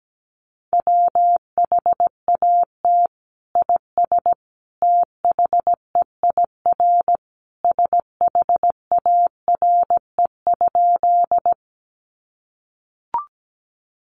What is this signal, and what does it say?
Morse code, heard as a keyed tone, about 17 wpm, 700 Hz. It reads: WHAT IS THEIR SHARE?
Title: Sentences from Top 900 Words 17wpm